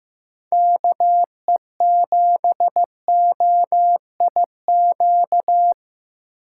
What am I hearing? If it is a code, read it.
Morse code, heard as a keyed tone, about 15 wpm, 700 Hz. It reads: KE7OIQ
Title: USA Call Signs - 6 Characters - 8x Character Spacing - 15wpm